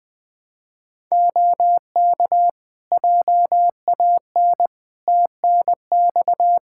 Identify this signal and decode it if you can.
Morse code, heard as a keyed tone, about 20 wpm, 700 Hz. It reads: OK JAN TNX
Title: K1USN SST QSO Practice 20wpm